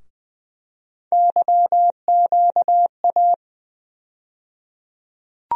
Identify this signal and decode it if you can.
Morse code, heard as a keyed tone, about 20 wpm, 700 Hz. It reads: YQA
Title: USA Call Sign Postfixes - 3 Letters 20wpm